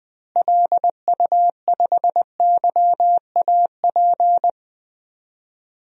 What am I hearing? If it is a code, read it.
Morse code, heard as a keyed tone, about 20 wpm, 700 Hz. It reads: LU5YAP
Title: DXCC-100 Call Signs - 6 Characters - Repeated 3x - 20wpm